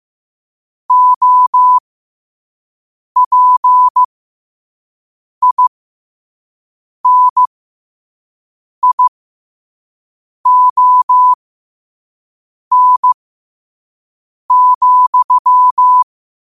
Morse code, heard as a keyed tone, about 15 wpm, spaced as for 5 wpm, 1000 Hz. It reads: OPINION,
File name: 5wpm